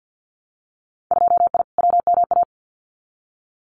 Morse code, heard as a keyed tone, about 50 wpm, 700 Hz. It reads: 2I WMA
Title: Winter Field Day Exchanges Encore 50wpm